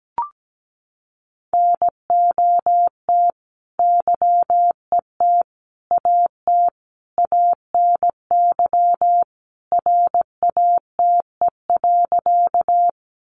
Morse code, heard as a keyed tone, about 17 wpm, 700 Hz. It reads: NOT YET AT ANY RATE.
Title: Sentences from Top 700 Words 17wpm